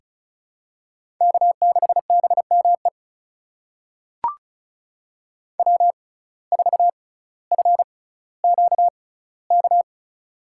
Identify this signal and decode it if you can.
Morse code, heard as a keyed tone, about 36 wpm, spaced as for 18 wpm, 700 Hz. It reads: K6BME W4FQK
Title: USA Call Signs - 5 Characters - 6x Character Spacing - 35wpm